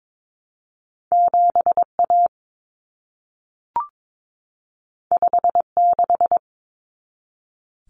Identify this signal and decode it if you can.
Morse code, heard as a keyed tone, about 22 wpm, 700 Hz. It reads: 7A 56
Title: Common Cut Numbers - 2-Digit 22wpm